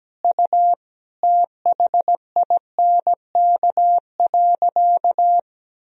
Morse code, heard as a keyed tone, about 17 wpm, 700 Hz. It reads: U THINK.
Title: Sentences from Top 400 Words and Abbreviations 17wpm